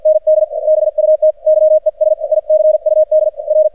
Morse code, heard as a keyed tone, about 51 wpm, spaced as for 44 wpm, 620 Hz, with radio noise. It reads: ND BUT YES ICUD R
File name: archived / CRSnet